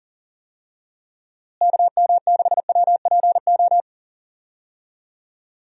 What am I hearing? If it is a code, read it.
Morse code, heard as a keyed tone, about 40 wpm, 700 Hz. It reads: KM6WPO
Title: USA Call Signs - 6 Characters - Repeated 2x - 40wpm